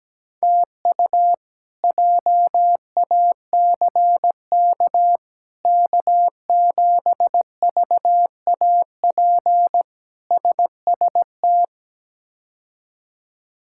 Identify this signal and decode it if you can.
Morse code, heard as a keyed tone, about 17 wpm, 700 Hz. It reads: TU JACK K7VAP SST
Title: K1USN SST QSO Practice - 17wpm